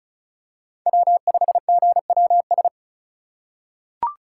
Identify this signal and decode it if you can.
Morse code, heard as a keyed tone, about 36 wpm, 690 Hz. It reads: W5GWS
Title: USA Call Signs - 5 Characters - 3x Character Spacing - 35wpm